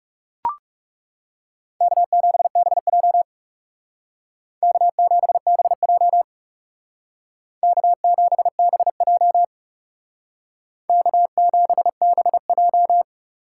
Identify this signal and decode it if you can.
Morse code, heard as a keyed tone, about 36 wpm, 700 Hz. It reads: K7BJ K7BJ K7BJ K7BJ
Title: Speed Racing - USA 1x2 Call Signs 30wpm